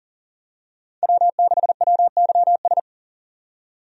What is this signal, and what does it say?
Morse code, heard as a keyed tone, about 40 wpm, 700 Hz. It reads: W6WYS
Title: USA Call Signs - 5 Characters - 40wpm